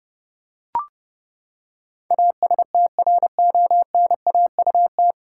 Morse code, heard as a keyed tone, about 30 wpm, 700 Hz. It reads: ASTRONAUT
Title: Pangram Challenge v2 - 30wpm